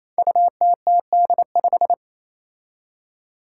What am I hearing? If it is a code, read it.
Morse code, heard as a keyed tone, about 28 wpm, 700 Hz. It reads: UTTD5
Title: Five Challenging Letter-Number Combinations 28wpm